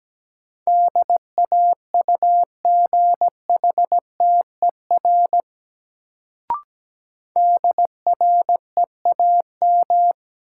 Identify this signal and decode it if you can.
Morse code, heard as a keyed tone, about 17 wpm, 700 Hz. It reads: DAUGHTER DREAM
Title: Sentences from Top 1500 Words - New Word Drill Alt - 17wpm